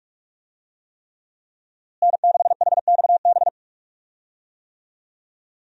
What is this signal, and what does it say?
Morse code, heard as a keyed tone, about 41 wpm, 700 Hz. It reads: N6HXB
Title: USA Call Signs - 5 Characters - Repeated 3x - 45wpm